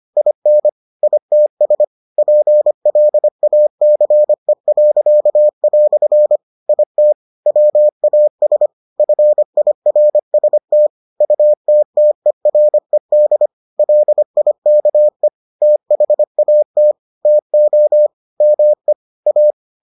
Morse code, heard as a keyed tone, about 25 wpm, 600 Hz. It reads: IN ITS PLACE." IT WAS FIRST UTTERED LIKE THAT TO ME A